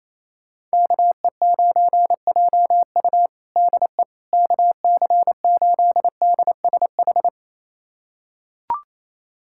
Morse code, heard as a keyed tone, about 28 wpm, 700 Hz. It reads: KE9JU DE KC8DSH
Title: USA Callsign DE Callsign 28wpm